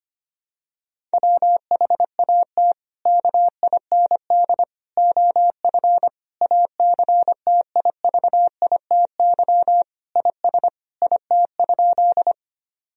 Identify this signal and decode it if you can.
Morse code, heard as a keyed tone, about 25 wpm, 700 Hz. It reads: WHAT KIND OF ACTIVITY IS IT?
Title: Sentences from Top 1000 Words - Review Part 4 - 25wpm